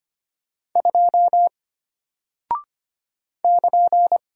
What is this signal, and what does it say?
Morse code, heard as a keyed tone, about 25 wpm, 700 Hz. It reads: 2 <KN>
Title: Single Character (Letter:Number:Punctuation:Prosign) Alt 25wpm